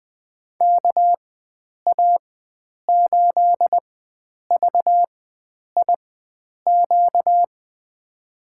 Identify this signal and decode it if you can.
Morse code, heard as a keyed tone, about 20 wpm, spaced as for 9 wpm, 700 Hz. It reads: KA8VIQ
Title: USA Call Signs - 6 Characters - 4x Character Spacing - 20wpm